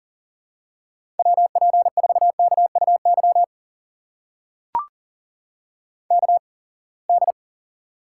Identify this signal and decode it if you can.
Morse code, heard as a keyed tone, about 40 wpm, spaced as for 28 wpm, 700 Hz. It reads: WP4KUY KD
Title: USA Call Signs - 8x Character Spacing - 40wpm